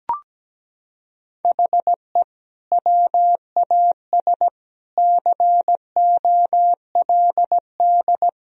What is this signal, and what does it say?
Morse code, heard as a keyed tone, about 17 wpm, 700 Hz. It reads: HE WAS COLD
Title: Sentences from Top 1000 Words - Review Part 4 - 17wpm